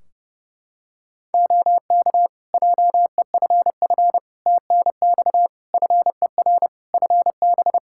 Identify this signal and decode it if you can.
Morse code, heard as a keyed tone, about 29 wpm, 700 Hz. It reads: OK JEFF TNX FER FB
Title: Practice QSO - Exchange 4 - Ending - 30wpm